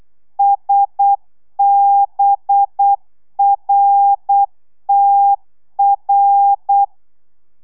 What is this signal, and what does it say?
Morse code, heard as a keyed tone, about 8 wpm, 800 Hz. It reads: SBRTR
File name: telegrafia / i2tfj / Lez6 35